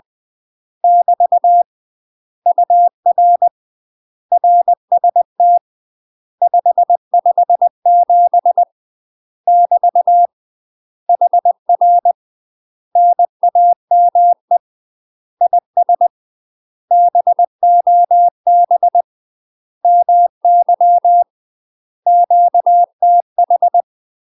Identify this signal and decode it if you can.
Morse code, heard as a keyed tone, about 20 wpm, 700 Hz. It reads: <BT> UR RST 557 <BT> HR NAME IS BOB MY QTH
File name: UK MASTER QSOs TRAINING / 20WPM / UK MASTER QSOs 0003